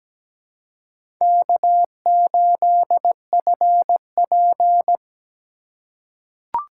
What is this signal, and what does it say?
Morse code, heard as a keyed tone, about 17 wpm, 700 Hz. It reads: K8FP
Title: USA Call Signs - 4 Characters - 3x Character Spacing - 17wpm